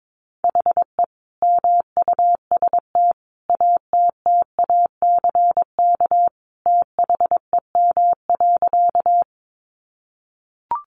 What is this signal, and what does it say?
Morse code, heard as a keyed tone, about 22 wpm, 700 Hz. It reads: HE MUST ATTACK THEM.
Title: Sentences from Top 1500 Words 22wpm